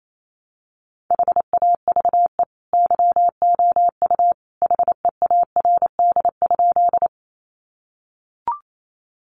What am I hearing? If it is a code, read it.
Morse code, heard as a keyed tone, about 28 wpm, 700 Hz. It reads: HAVE YOU HEARD?
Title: Sentences from Top 400 Words 28wpm